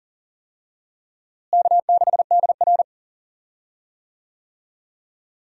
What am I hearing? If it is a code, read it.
Morse code, heard as a keyed tone, about 37 wpm, 700 Hz. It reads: K6DR
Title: USA Call Signs - 4 Characters - Repeated 2x - 40wpm